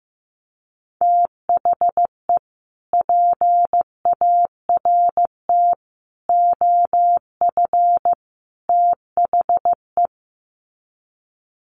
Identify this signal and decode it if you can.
Morse code, heard as a keyed tone, about 15 wpm, 700 Hz. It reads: THE PART OF THE
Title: Sets of 4 Words 15wpm